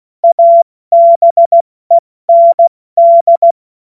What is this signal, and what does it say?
Morse code, heard as a keyed tone, about 16 wpm, 680 Hz. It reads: ABEND
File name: Witz-16-September-2025-70-F80